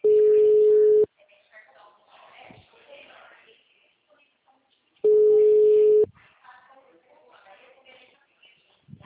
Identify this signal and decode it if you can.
Ringback tone (Europe, 425 Hz), with speech.